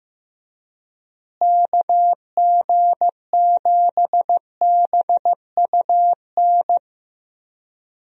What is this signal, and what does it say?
Morse code, heard as a keyed tone, about 15 wpm, 700 Hz. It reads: KG7BUN
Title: USA Call Signs - 6 Characters - Repeated 2x - 15wpm